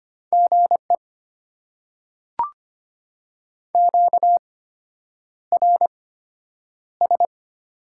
Morse code, heard as a keyed tone, about 25 wpm, spaced as for 9 wpm, 690 Hz. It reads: GE QRS
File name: Top 90 QSO Elements - 8x Character Spacing - 25wpm